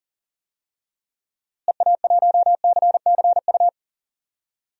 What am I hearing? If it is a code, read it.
Morse code, heard as a keyed tone, about 40 wpm, 700 Hz. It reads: EA1CCU
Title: DXCC-100 Call Signs - 6 Characters - Repeated 3x - 40wpm